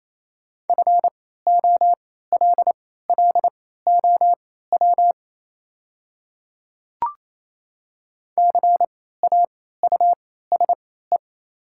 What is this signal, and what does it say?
Morse code, heard as a keyed tone, about 28 wpm, spaced as for 16 wpm, 700 Hz. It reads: FOLLOW CAUSE
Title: 3x Character Spacing - Top 300 Words - 28wpm